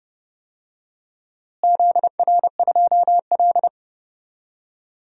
Morse code, heard as a keyed tone, about 30 wpm, 700 Hz. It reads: ZR2L
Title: DXCC-100 Call Signs - 4 Characters - Repeated 2x - 30wpm